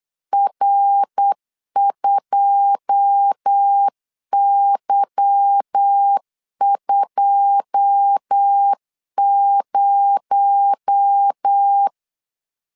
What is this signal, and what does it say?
Morse code, heard as a keyed tone, about 8 wpm, 800 Hz. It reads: R2Y20
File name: LEZIONI / 9 2 8wpm 3437